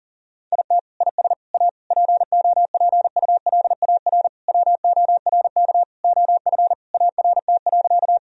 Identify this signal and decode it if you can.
Morse code, heard as a keyed tone, about 40 wpm, 700 Hz. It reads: IT IS A POPULAR WORK OF ART.
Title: Sentences from Top 1000 Words - Review Part 3 - 40wpm